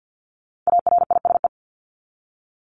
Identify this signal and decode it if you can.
Morse code, heard as a keyed tone, about 48 wpm, 700 Hz. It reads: ARISE
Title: Top 1501-1750 Words 50wpm